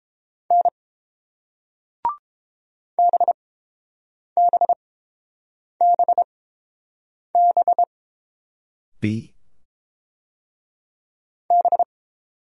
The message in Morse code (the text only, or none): N BBBBB